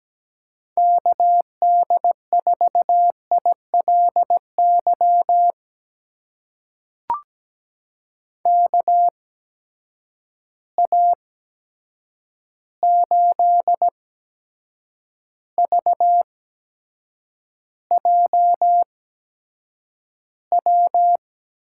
Morse code, heard as a keyed tone, about 17 wpm, spaced as for 7 wpm, 700 Hz. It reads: KD4ILY KA8VJW
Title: USA Call Signs - 6 Characters - 8x Character Spacing - 17wpm